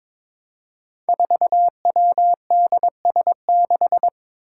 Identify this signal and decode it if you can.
Morse code, heard as a keyed tone, about 22 wpm, 700 Hz. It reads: 4WDS6
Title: Five Challenging Letter-Number Combinations 22wpm